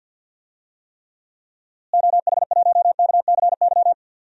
Morse code, heard as a keyed tone, about 46 wpm, 700 Hz. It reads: OH1KCY
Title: DXCC-100 Call Signs - 6 Characters - Repeated 3x - 50wpm